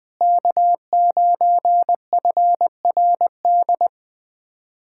Morse code, heard as a keyed tone, about 21 wpm, 700 Hz. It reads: K9FRD
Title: USA Callsign DE Callsign 20wpm